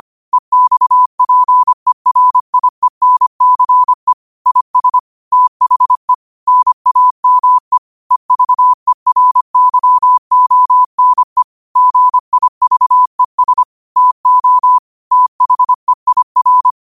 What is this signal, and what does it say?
Morse code, heard as a keyed tone, about 25 wpm, 1000 Hz. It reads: EXPERIENCE IS THE NAME EVERYONE GIVES TO THEIR
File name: qotdfiles / 25wpm